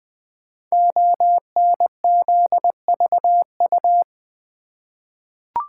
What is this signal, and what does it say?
Morse code, heard as a keyed tone, about 20 wpm, 700 Hz. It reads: ONZVU
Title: Code Groups - 5 Letter:Numbers - 20wpm